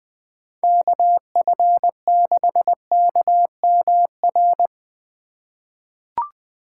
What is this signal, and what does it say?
Morse code, heard as a keyed tone, about 20 wpm, 700 Hz. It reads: KF6KMR
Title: USA Call Signs - 6 Characters - 4x Character Spacing - 20wpm